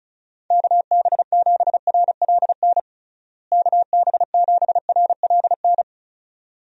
Morse code, heard as a keyed tone, about 36 wpm, 700 Hz. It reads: KB7RLN KB7RLN
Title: USA Call Signs - 6 Characters - Repeated 2x - 35wpm